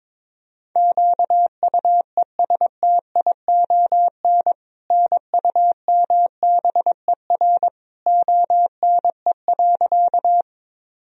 Morse code, heard as a keyed tone, about 22 wpm, 700 Hz. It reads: QUESTION NUMBER ONE.
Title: Sentences from Top 1000 Words - Review Part 1 - 22wpm